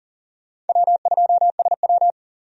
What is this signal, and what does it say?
Morse code, heard as a keyed tone, about 40 wpm, 700 Hz. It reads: W2SW
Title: USA Call Signs - 4 Characters - 40wpm